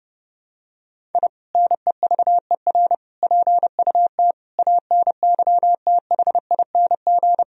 Morse code, heard as a keyed tone, about 30 wpm, 700 Hz. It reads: I NEVER PUT ANYTHING
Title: Sentences from Top 1000 Words - Review Part 1 - 30wpm